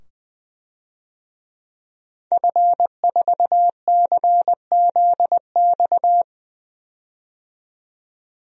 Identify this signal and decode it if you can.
Morse code, heard as a keyed tone, about 20 wpm, 700 Hz. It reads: F4CZX
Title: DXCC-100 Call Signs - 5 Characters - 20wpm